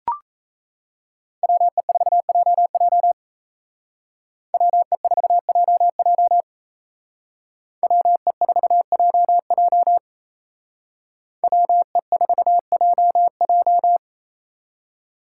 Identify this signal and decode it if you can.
Morse code, heard as a keyed tone, about 36 wpm, 700 Hz. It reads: WE4JJ WE4JJ WE4JJ WE4JJ